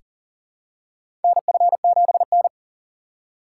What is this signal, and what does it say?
Morse code, heard as a keyed tone, about 40 wpm, 700 Hz. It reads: NF7N
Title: USA Call Signs - 4 Characters - 4x Character Spacing - 40wpm